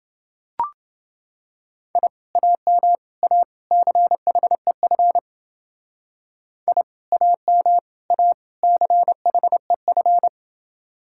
Morse code, heard as a keyed tone, about 29 wpm, 700 Hz. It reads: I AM A CHEF I AM A CHEF